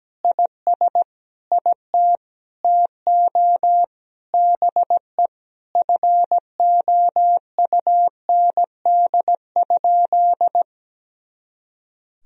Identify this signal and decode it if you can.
Morse code, heard as a keyed tone, about 17 wpm, 700 Hz. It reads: IS IT TO BE FOUND?
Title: Sentences from Top 300 Words and Abbreviations 17wpm